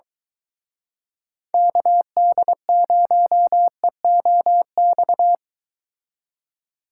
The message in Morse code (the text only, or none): KD0EOX